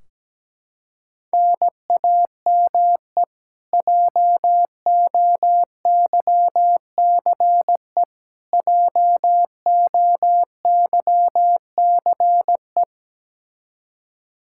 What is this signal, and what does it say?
Morse code, heard as a keyed tone, about 17 wpm, 700 Hz. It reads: NAME JOYCE JOYCE
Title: Practice QSO - Exchange 1 - Beginning - 17wpm